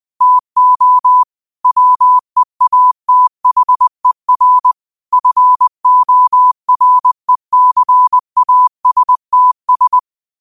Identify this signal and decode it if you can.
Morse code, heard as a keyed tone, about 20 wpm, 1000 Hz. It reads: TO WEATHER FORECASTS